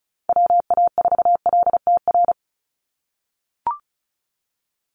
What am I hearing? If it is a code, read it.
Morse code, heard as a keyed tone, about 34 wpm, 700 Hz. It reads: WA4LTR
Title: USA Call Signs 35wpm